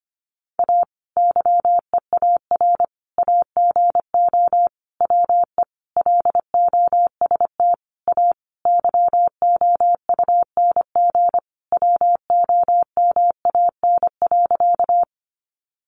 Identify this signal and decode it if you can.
Morse code, heard as a keyed tone, about 25 wpm, 700 Hz. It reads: A YEAR AGO WE LOST A YOUNG WOMAN.
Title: Sentences from Top 1400 Words 25wpm